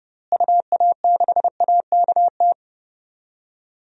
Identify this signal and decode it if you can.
Morse code, heard as a keyed tone, about 30 wpm, 700 Hz. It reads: UA6AKT
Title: DXCC-100 Call Signs - 6 Characters - Repeated 3x - 30wpm